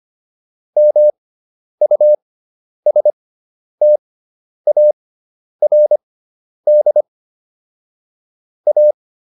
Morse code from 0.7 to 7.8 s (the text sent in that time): MUSTARD